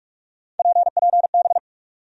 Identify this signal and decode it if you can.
Morse code, heard as a keyed tone, about 42 wpm, 700 Hz. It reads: PPB
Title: USA Call Sign Postfixes - 3 Letters 45wpm